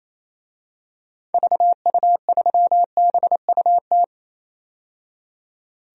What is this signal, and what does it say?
Morse code, heard as a keyed tone, about 28 wpm, 700 Hz. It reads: VU3BUT
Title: DXCC-100 Call Signs - 6 Characters - 28wpm